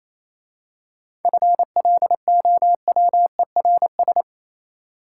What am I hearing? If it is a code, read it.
Morse code, heard as a keyed tone, about 27 wpm, 700 Hz. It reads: FLOWERS
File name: Easy Words - Speed-Racing with Hints - 22wpm